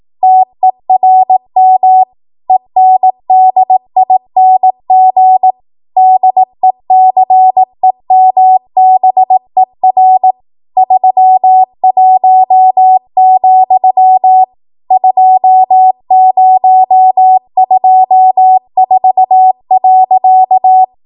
Morse code, heard as a keyed tone, about 18 wpm, 750 Hz. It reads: TERM ENDING DECEMBER 31, 2024.
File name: Archive / 18 WPM / 18WPM